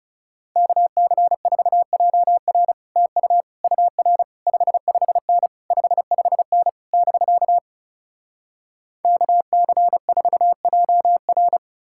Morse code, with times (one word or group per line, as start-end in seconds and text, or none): 0.6-2.7 KC4JR
3.0-3.4 TU
3.6-4.2 UR
4.5-5.5 55N
5.7-6.7 55N
6.9-7.6 <BK>
9.0-11.6 KC4JR